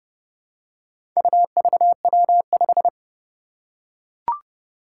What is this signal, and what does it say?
Morse code, heard as a keyed tone, about 30 wpm, 700 Hz. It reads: UVW5
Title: Four Challenging Letter-Number Combinations 30wpm